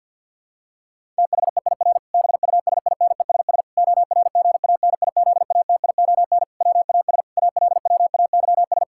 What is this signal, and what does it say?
Morse code, heard as a keyed tone, about 50 wpm, 700 Hz. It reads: THEIR BUSINESS ORGANIZATION WAS ALWAYS
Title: Sentences from Top 1000 Words - Review Part 5 - 50wpm